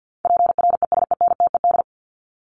Morse code, heard as a keyed tone, about 48 wpm, 700 Hz. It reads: PRESENTED
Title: ED Suffix - Top 100 Words with Base Words 50wpm